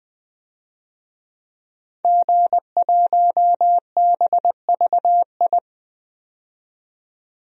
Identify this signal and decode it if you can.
Morse code, heard as a keyed tone, about 20 wpm, 700 Hz. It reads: G1BVI
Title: DXCC-100 Call Signs - 5 Characters - 20wpm